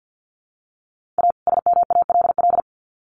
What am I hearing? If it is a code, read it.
Morse code, heard as a keyed tone, about 50 wpm, 700 Hz. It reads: A SMALL
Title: Sets of 2 Words 50wpm